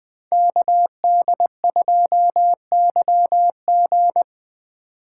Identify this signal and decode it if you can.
Morse code, heard as a keyed tone, about 20 wpm, 700 Hz. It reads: KD2YG
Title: Speed Racing - USA 2x2 Call Signs 17wpm